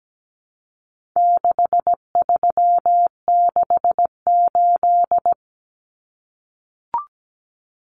Morse code, heard as a keyed tone, about 17 wpm, 700 Hz. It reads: 6368
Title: Common Cut Numbers - 4-Digit 17wpm